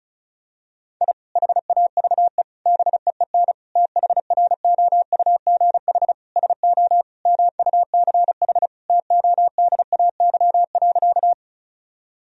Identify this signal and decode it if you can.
Morse code, heard as a keyed tone, about 36 wpm, 700 Hz. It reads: I HAVE BEEN THROUGH SO MUCH TODAY.
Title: Sentences from Top 1000 Words - Review Part 4 - 35wpm